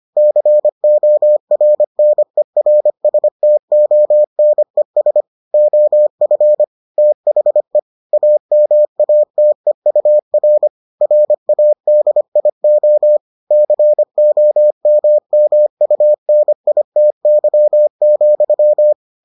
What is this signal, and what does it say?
Morse code, heard as a keyed tone, about 25 wpm, 600 Hz. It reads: CORNERSTONES OF THE AMATEUR RADIO COMMUNITY,